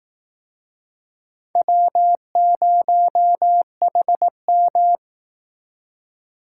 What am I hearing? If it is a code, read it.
Morse code, heard as a keyed tone, about 18 wpm, 700 Hz. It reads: W0HM
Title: Speed Racing - USA 1x2 Call Signs 15wpm